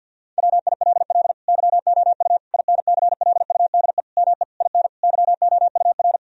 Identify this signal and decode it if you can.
Morse code, heard as a keyed tone, about 50 wpm, 700 Hz. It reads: WILL YOU INCLUDE ME IN YOUR
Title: Sentences from Top 1000 Words - Review Part 3 - 50wpm